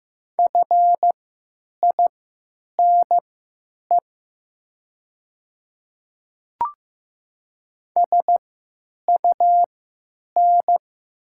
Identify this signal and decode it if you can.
Morse code, heard as a keyed tone, about 15 wpm, spaced as for 8 wpm, 700 Hz. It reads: FINE SUN
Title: 3x Character Spacing - Top 500 Words - 15wpm